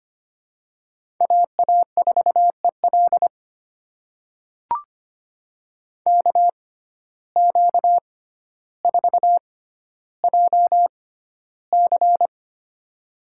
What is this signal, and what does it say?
Morse code, heard as a keyed tone, about 25 wpm, spaced as for 13 wpm, 700 Hz. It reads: AA4EL KQ4JC
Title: USA Call Signs - 5 Characters - 6x Character Spacing - 25wpm